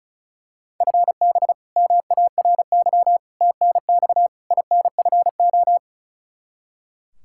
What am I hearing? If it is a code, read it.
Morse code, heard as a keyed tone, about 36 wpm, 700 Hz. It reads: FB MARY TNX INFO
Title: Practice QSO - Exchange 3 - More Info - 35wpm